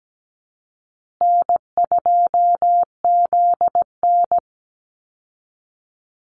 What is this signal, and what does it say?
Morse code, heard as a keyed tone, about 17 wpm, 700 Hz. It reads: N2ZN
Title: Call Signs Repeated 2x - 17wpm